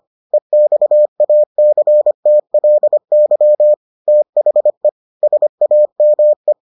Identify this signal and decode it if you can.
Morse code, heard as a keyed tone, about 25 wpm, 600 Hz. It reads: EXACTLY THE SAME